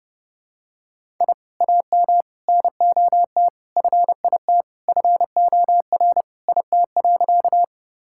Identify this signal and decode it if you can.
Morse code, heard as a keyed tone, about 32 wpm, 700 Hz. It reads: I AM NOT FIT FOR IT.
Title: Sentences from Top 1000 Words - Review Part 5 - 30wpm